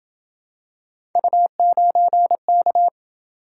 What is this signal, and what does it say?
Morse code, heard as a keyed tone, about 27 wpm, 700 Hz. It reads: U9K